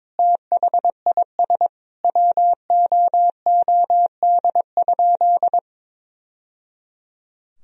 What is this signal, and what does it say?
Morse code, heard as a keyed tone, about 22 wpm, 700 Hz. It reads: THIS WOOD?
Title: Sentences from Top 1000 Words - Review Part 2 - 22wpm